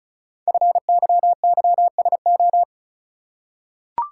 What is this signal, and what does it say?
Morse code, heard as a keyed tone, about 35 wpm, 700 Hz. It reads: FYYSO
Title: Code Groups - 5 Letters - 35wpm